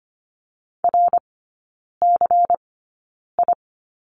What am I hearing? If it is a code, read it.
Morse code, heard as a keyed tone, about 25 wpm, spaced as for 9 wpm, 700 Hz. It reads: RCI
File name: course / 25:10wpm / 093 Lesson 16e - Letter c - Three Characters 25wpm